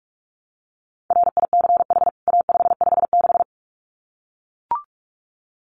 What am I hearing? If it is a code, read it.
Morse code, heard as a keyed tone, about 44 wpm, 700 Hz. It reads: RICH A556
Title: CWT Exchange 45wpm